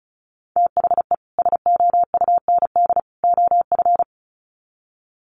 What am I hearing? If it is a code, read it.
Morse code, heard as a keyed tone, about 34 wpm, 700 Hz. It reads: THE SOUND OF
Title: Sets of 3 Words 35wpm